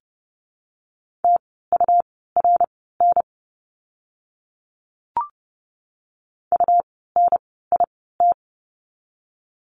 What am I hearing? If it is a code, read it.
Morse code, heard as a keyed tone, about 30 wpm, spaced as for 17 wpm, 700 Hz. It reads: TURN UNIT
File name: Lesson 17f - Letter u - Words 30wpm